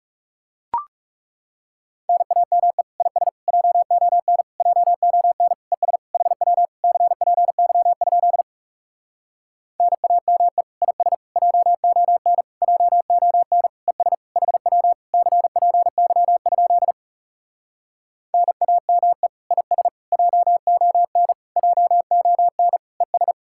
Morse code, heard as a keyed tone, about 40 wpm, 700 Hz. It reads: NAME IS JON JON ES HW CPY? NAME IS JON JON ES HW CPY? NAME IS JON JON ES